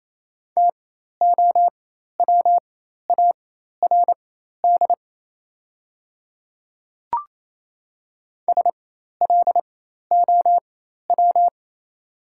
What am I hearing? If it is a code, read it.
Morse code, heard as a keyed tone, about 28 wpm, spaced as for 13 wpm, 700 Hz. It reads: TOWARD SLOW
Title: 4x Character Spacing - Top 500 Words - 28wpm